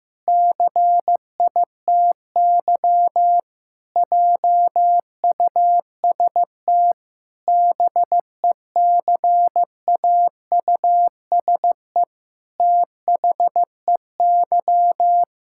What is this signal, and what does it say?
Morse code, heard as a keyed tone, about 14 wpm, 700 Hz. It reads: CITY JUST BECAUSE THEY
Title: Pangram Challenge v2 - 15wpm